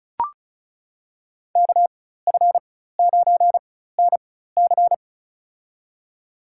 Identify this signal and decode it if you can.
Morse code, heard as a keyed tone, about 34 wpm, spaced as for 16 wpm, 700 Hz. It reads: KF9NC
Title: USA Call Signs - 5 Characters - 4x Character Spacing - 35wpm